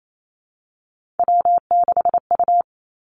Morse code, heard as a keyed tone, about 28 wpm, 700 Hz. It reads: W6U
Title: Three Challenging Letter-Number Combinations 28wpm